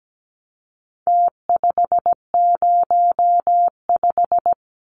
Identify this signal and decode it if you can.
Morse code, heard as a keyed tone, about 17 wpm, 700 Hz. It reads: T505
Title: Numbers - Contest Style - 4-Digit 17wpm